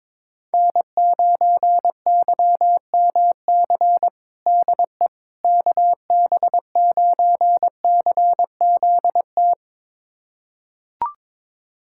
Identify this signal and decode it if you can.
Morse code, heard as a keyed tone, about 22 wpm, 700 Hz. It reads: N9YMC DE KB9CZT
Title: USA Callsign DE Callsign 22wpm